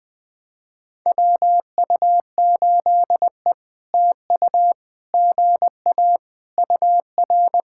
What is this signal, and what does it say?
Morse code, heard as a keyed tone, about 20 wpm, 700 Hz. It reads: WU8E TU GA UR
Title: POTA QSO Practice - 20wpm